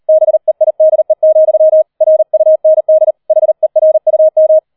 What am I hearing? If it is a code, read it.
Morse code, heard as a keyed tone, about 38 wpm, 630 Hz. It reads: BEIDE, RUND HERUM